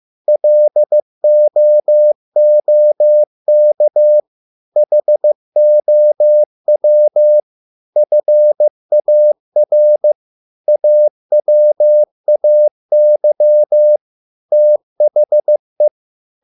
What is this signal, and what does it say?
Morse code, heard as a keyed tone, about 15 wpm, 600 Hz. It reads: LOOK HOW FAR AWAY THE